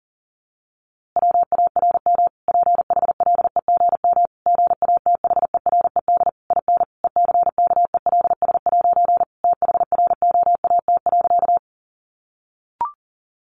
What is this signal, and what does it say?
Morse code, heard as a keyed tone, about 37 wpm, 700 Hz. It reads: WARM PHLEGM GATHERED IN ECKELS' THROAT.
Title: A Sound Of Thunder 40wpm